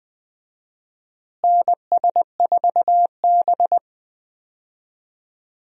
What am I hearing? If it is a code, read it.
Morse code, heard as a keyed tone, about 20 wpm, 700 Hz. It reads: NS4B
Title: USA Call Signs - 4 Characters - Repeated 3x - 20wpm